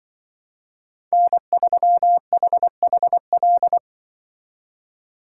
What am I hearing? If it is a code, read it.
Morse code, heard as a keyed tone, about 23 wpm, 700 Hz. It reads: N3HHL
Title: Speed Racing - USA 1x3 Call Signs 20wpm